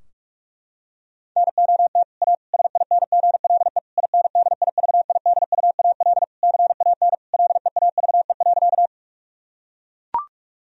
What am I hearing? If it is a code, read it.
Morse code, heard as a keyed tone, about 44 wpm, 700 Hz. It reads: NOT A SINGLE INDIVIDUAL CAN LEAVE.
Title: Sentences from Top 1000 Words - Review Part 4 - 45wpm